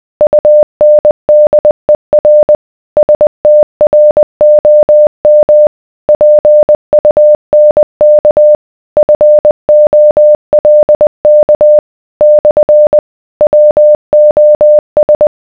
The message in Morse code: UNDER STROM PUNK FOLK / WOS